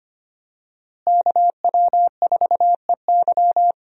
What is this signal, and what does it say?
Morse code, heard as a keyed tone, about 25 wpm, 700 Hz. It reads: KW4EY
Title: USA Call Signs - 5 Characters - Repeated 3x - 25wpm